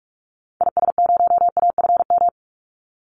Morse code, heard as a keyed tone, about 44 wpm, 700 Hz. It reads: IS0AFM
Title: Call Signs - 45wpm